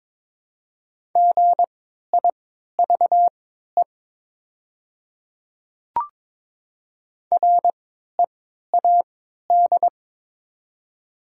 Morse code, heard as a keyed tone, about 22 wpm, spaced as for 13 wpm, 700 Hz. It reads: GIVE READ
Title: 3x Character Spacing - Top 200 Words - 22wpm